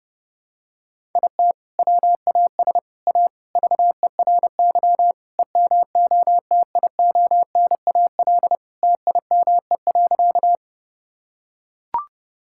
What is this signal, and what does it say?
Morse code, heard as a keyed tone, about 32 wpm, 700 Hz. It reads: IT WAS A VERY EMOTIONAL TIME.
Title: Sentences from Top 1000 Words - Review Part 5 - 30wpm